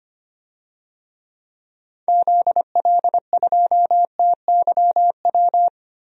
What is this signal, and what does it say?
Morse code, heard as a keyed tone, about 25 wpm, 700 Hz. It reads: ZL2TYW
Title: DXCC-100 Call Signs - 6 Characters - Repeated 2x - 25wpm